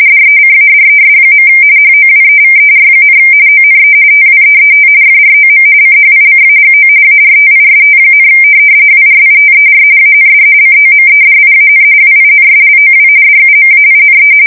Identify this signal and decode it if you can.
RTTY, about 45.45 baud. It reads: DECADES. HOWEVER, ONE TWEET FROM CHICAGO ARTIST, WRITER AND COMEDIAN WINSLOW DUMAINE
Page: 170 Shift 2125/2295